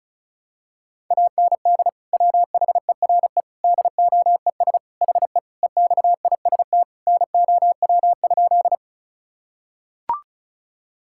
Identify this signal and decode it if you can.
Morse code, heard as a keyed tone, about 34 wpm, 700 Hz. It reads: AND WHERE DOES HE EXIST NOW?
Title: Sentences from Top 1000 Words - Review Part 5 - 35wpm